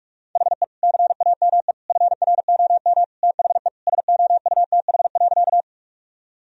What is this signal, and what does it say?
Morse code, heard as a keyed tone, about 46 wpm, 700 Hz. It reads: HE CAME FROM THE SOUTH.
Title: Sentences from Top 1000 Words - Review Part 2 - 45wpm